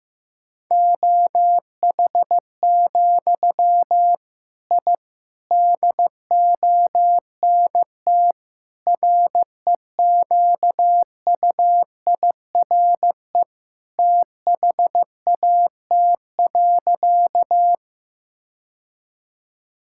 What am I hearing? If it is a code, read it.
Morse code, heard as a keyed tone, about 15 wpm, 700 Hz. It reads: OH, I DONT REQUIRE THAT.
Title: Sentences from Top 1000 Words - Review Part 6 - 15wpm